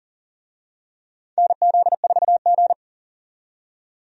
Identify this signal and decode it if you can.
Morse code, heard as a keyed tone, about 40 wpm, 700 Hz. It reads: NZ4G
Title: USA Call Signs - 4 Characters - 40wpm